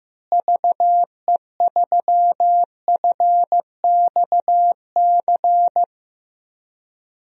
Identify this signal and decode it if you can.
Morse code, heard as a keyed tone, about 15 wpm, 700 Hz. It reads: VE3FXC
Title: DXCC-100 Call Signs - 6 Characters - Repeated 3x - 15wpm